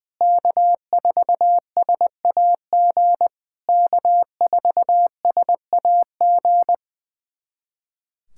Morse code, heard as a keyed tone, about 20 wpm, 700 Hz. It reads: K4SAG K4SAG
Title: SOTA QSO Practice 20wpm